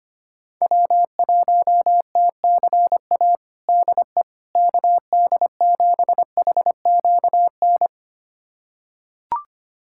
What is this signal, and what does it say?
Morse code, heard as a keyed tone, about 25 wpm, 700 Hz. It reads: W1TCA DE KD7HQN
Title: USA Callsign DE Callsign 25wpm